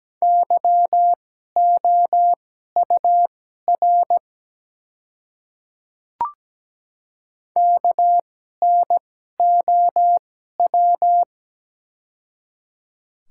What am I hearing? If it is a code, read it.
Morse code, heard as a keyed tone, about 17 wpm, spaced as for 12 wpm, 700 Hz. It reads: YOUR KNOW
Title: 2x Character Spacing - Top 300 Words - 17wpm